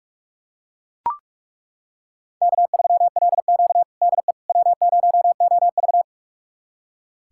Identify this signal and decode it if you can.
Morse code, heard as a keyed tone, about 43 wpm, 700 Hz. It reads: K3LQ DE W0OV
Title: USA Callsign DE Callsign - 4 Characters - 45wpm